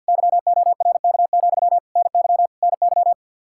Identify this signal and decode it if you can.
Morse code, heard as a keyed tone, about 46 wpm, 700 Hz. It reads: YORK, NY NY
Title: Practice QSO - Exchange 1 - Beginning - 50wpm